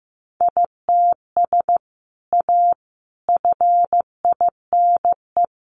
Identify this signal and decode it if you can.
Morse code, heard as a keyed tone, about 15 wpm, 700 Hz. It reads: ITS A FINE
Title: Sentences from Top 600 Words 15wpm